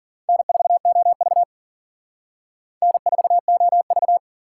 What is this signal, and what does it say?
Morse code, heard as a keyed tone, about 49 wpm, 690 Hz. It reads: N4OV N4OV
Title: Speed Racing - USA Call Signs 40wpm